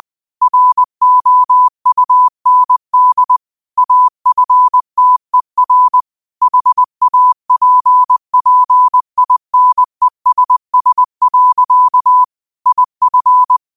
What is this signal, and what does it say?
Morse code, heard as a keyed tone, about 20 wpm, 1000 Hz. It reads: ROUND AFTER HAPPINESS. IF
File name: qotdfiles / 20wpm